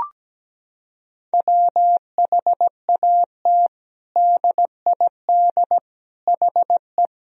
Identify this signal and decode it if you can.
Morse code, heard as a keyed tone, about 17 wpm, 700 Hz. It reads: WHAT DID HE
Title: Sentences from Top 1000 Words - Review Part 4 - 17wpm